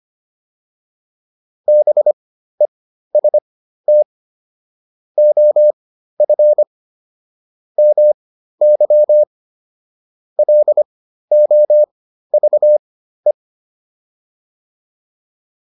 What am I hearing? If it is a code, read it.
Morse code, heard as a keyed tone, about 25 wpm, spaced as for 13 wpm, 600 Hz. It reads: BEST OF MY LOVE